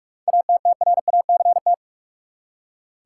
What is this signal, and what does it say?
Morse code, heard as a keyed tone, about 46 wpm, 700 Hz. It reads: ATTRACT
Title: Seven Letter Words - 45wpm